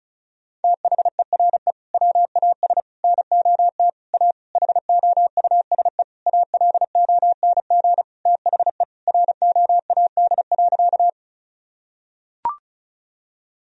Morse code, heard as a keyed tone, about 34 wpm, 700 Hz. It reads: THERE WAS NOT A HOUSE ALONG THE ROAD.
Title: Sentences from Top 1000 Words - Review Part 2 - 35wpm